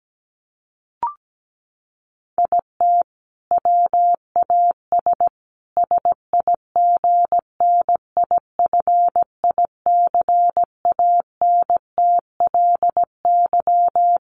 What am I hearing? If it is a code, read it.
Morse code, heard as a keyed tone, about 17 wpm, 700 Hz. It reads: IT WAS SIGNIFICANTLY
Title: Sentences from Top 1500 Words 17wpm